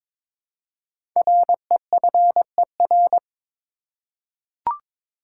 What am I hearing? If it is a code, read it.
Morse code, heard as a keyed tone, about 22 wpm, 700 Hz. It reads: REFER
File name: 5 Letter Words - First 3 Letters Given - 22wpm